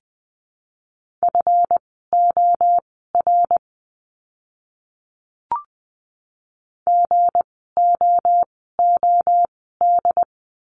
Morse code, heard as a keyed tone, about 20 wpm, spaced as for 14 wpm, 700 Hz. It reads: FOR GOOD
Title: 244 Lesson 39f - Letter z - Words 20wpm